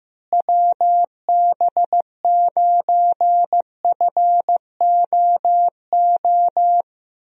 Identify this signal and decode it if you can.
Morse code, heard as a keyed tone, about 16 wpm, 700 Hz. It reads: WB9FOO
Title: USA Call Signs - 6 Characters - Repeated 3x - 15wpm